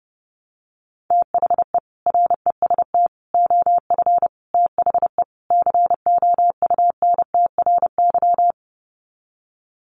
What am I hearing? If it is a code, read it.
Morse code, heard as a keyed tone, about 30 wpm, 700 Hz. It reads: THE REST OF THE COUNTRY
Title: Sets of 5 Words 30wpm